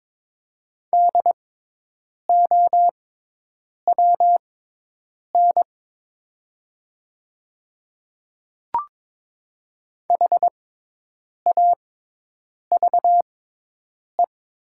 Morse code, heard as a keyed tone, about 22 wpm, spaced as for 8 wpm, 700 Hz. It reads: DOWN HAVE
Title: 6x Character Spacing - Top 200 Words - 22wpm